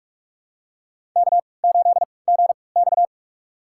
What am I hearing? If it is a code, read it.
Morse code, heard as a keyed tone, about 46 wpm, spaced as for 26 wpm, 700 Hz. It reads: K8GX
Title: USA Call Signs - 3x Character Spacing - 45wpm